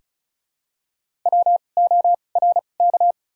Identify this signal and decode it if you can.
Morse code, heard as a keyed tone, about 34 wpm, spaced as for 25 wpm, 700 Hz. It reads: WORK
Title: 2x Character Spacing - Top 300 Words - 35wpm